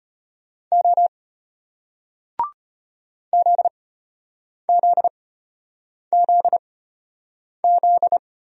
Morse code, heard as a keyed tone, about 34 wpm, spaced as for 8 wpm, 700 Hz. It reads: O ZZZZ